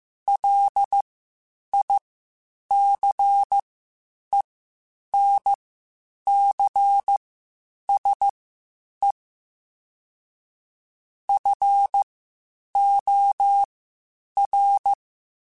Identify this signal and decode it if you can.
Morse code, heard as a keyed tone, about 15 wpm, spaced as for 8 wpm, 800 Hz. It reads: LICENCSE FOR